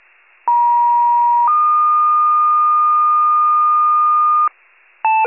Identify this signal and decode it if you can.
Two-tone page, heard as a one-second tone, then a three-second tone, about 950 then 1220 Hz, with noise. It is heard from 0.5 to 4.5 s.